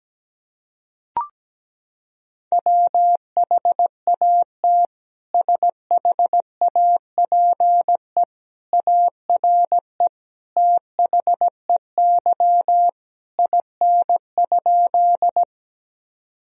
Morse code, heard as a keyed tone, about 17 wpm, 700 Hz. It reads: WHAT SHAPE ARE THEY IN?
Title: Sentences from Top 1000 Words - Review Part 3 - 17wpm